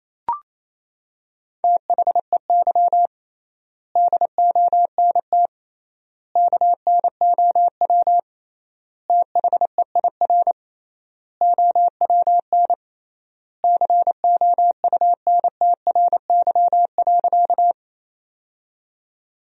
Morse code, heard as a keyed tone, about 28 wpm, 700 Hz. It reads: THEY DONT KNOW THEIR OWN COUNTRY.
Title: 3x Word-Spacing - Sentences from Top 300 Words - 28wpm